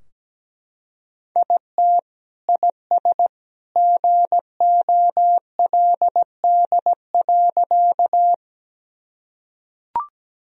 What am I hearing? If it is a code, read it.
Morse code, heard as a keyed tone, about 17 wpm, 700 Hz. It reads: IT IS GOLD.
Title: Sentences from Top 1000 Words - Review Part 3 - 17wpm